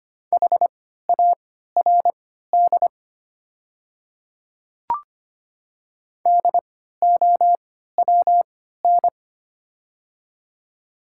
Morse code, heard as a keyed tone, about 25 wpm, spaced as for 14 wpm, 700 Hz. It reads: HARD DOWN
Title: 3x Character Spacing - Top 300 Words - 25wpm